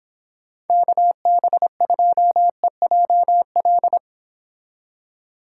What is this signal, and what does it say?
Morse code, heard as a keyed tone, about 31 wpm, 700 Hz. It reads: KB2EJL